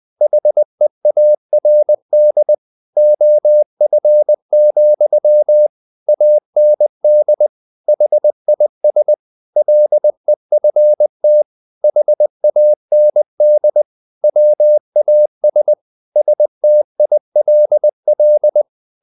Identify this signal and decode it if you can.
Morse code, heard as a keyed tone, about 20 wpm, 600 Hz. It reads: HEARD OF, AND HIS LEFT HAND WAS STILL
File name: the old man and the sea / the old man and the sea part 1 20wpm